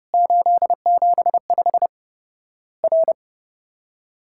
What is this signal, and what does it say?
Morse code, heard as a keyed tone, about 30 wpm, 700 Hz. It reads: 875 R
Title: Worked All Europe DX Contest - QTC Exchange - 30wpm